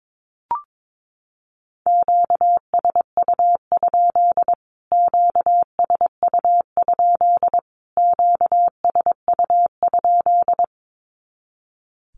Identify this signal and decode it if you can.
Morse code, heard as a keyed tone, about 22 wpm, 700 Hz. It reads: QSU? QSU? QSU?
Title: Q-Codes Repeated 3x 22wpm